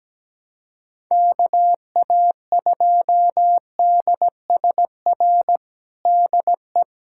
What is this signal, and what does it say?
Morse code, heard as a keyed tone, about 17 wpm, 700 Hz. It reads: KA2DSR DE
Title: USA Callsign DE Callsign 17wpm